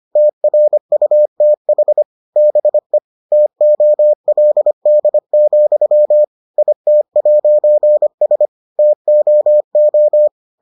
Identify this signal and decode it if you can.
Morse code, heard as a keyed tone, about 25 wpm, 600 Hz. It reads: TRUTH BE TOLD, IT'S TOO